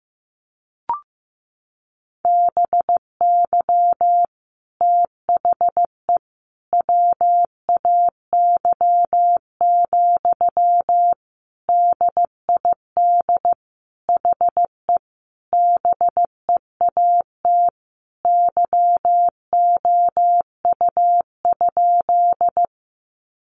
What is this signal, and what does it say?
Morse code, heard as a keyed tone, about 15 wpm, 700 Hz. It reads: BY THE WAY, DID HE BEAT YOU?
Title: Sentences from Top 1300 Words 15wpm